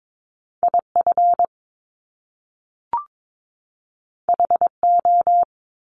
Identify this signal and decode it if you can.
Morse code, heard as a keyed tone, about 22 wpm, 700 Hz. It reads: IF HO
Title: Top Two Letter Combinations in Words 22wpm